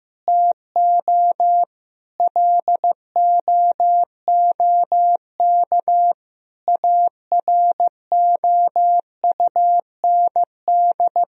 Morse code, heard as a keyed tone, about 15 wpm, 700 Hz. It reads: TO LOOK AROUND
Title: Sentences from Top 1000 Words - Review Part 3 - 15wpm